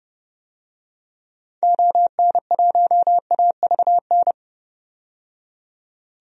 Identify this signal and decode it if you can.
Morse code, heard as a keyed tone, about 29 wpm, 700 Hz. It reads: ON1AVN
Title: DXCC-100 Call Signs - 6 Characters - 30wpm